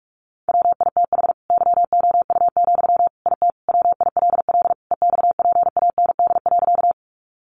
Morse code, heard as a keyed tone, about 44 wpm, 700 Hz. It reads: WITH YOU, IT WILL EXPAND.
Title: Sentences from Top 1400 Words 45wpm